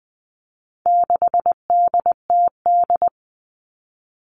Morse code, heard as a keyed tone, about 20 wpm, 700 Hz. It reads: 6DTD
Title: All Cut Numbers - 4-Digit 20wpm